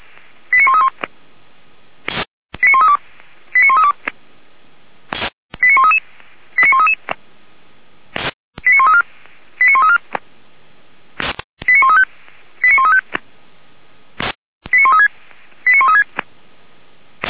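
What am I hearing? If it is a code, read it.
Five-tone selective call (ZVEI), heard as quick runs of five tones, tones of about 70 ms. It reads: 89131 89132 89132 89133 89133 89134 89134 89135 89135 89136 89136
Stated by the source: ZVEI five-tone call 89110 received by SDR.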